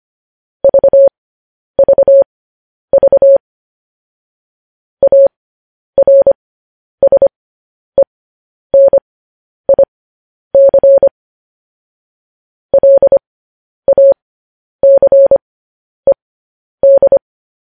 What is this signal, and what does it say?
Morse code, heard as a keyed tone, about 25 wpm, spaced as for 10 wpm, 550 Hz. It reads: VVV ARSENIC LACED
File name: podcast / News headlines-1x-25x10